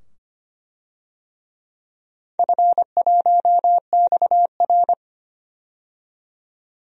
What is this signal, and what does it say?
Morse code, heard as a keyed tone, about 25 wpm, 700 Hz. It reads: F1XR
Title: DXCC-100 Call Signs - 4 Characters - 25wpm